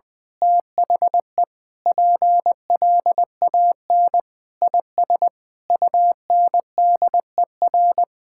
Morse code, heard as a keyed tone, about 20 wpm, 700 Hz. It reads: THE PLAN IS UNDER
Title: Sentences with 9-Word Phrases - Spoken Beginning - 20wpm